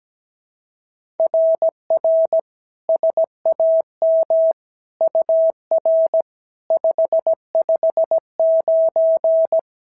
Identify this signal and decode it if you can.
Morse code, heard as a keyed tone, about 16 wpm, 650 Hz. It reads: RR SAM UR 559
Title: SOTA QSO Practice 17wpm